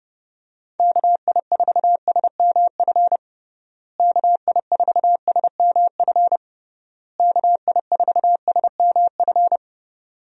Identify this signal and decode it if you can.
Morse code, heard as a keyed tone, about 30 wpm, 700 Hz. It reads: KI4SMF KI4SMF KI4SMF
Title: USA Call Signs - 6 Characters - Repeated 3x - 30wpm